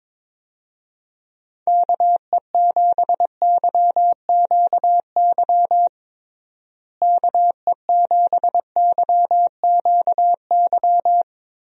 Morse code, heard as a keyed tone, about 22 wpm, 700 Hz. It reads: KE7YQY KE7YQY
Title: USA Call Signs - 6 Characters - Repeated 3x - 22wpm